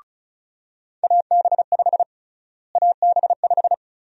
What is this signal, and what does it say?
Morse code, heard as a keyed tone, about 36 wpm, 700 Hz. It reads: AB5 AB5
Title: USA Call Sign Prefixes - Repeated 2x - 35wpm